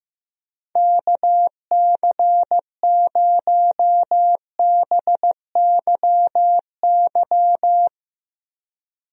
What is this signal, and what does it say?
Morse code, heard as a keyed tone, about 15 wpm, 700 Hz. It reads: KC0BYY
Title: USA Call Signs - 6 Characters - Repeated 2x - 15wpm